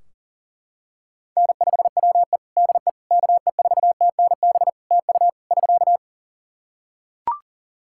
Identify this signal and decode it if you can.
Morse code, heard as a keyed tone, about 40 wpm, 700 Hz. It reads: N5WE DE KE4TNB TU <SK>
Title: K1USN SST QSO Practice 40wpm